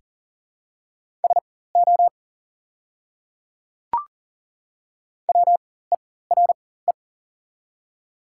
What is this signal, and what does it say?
Morse code, heard as a keyed tone, about 40 wpm, spaced as for 19 wpm, 700 Hz. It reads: SO WERE
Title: 4x Character Spacing - Top 100 Words - 40wpm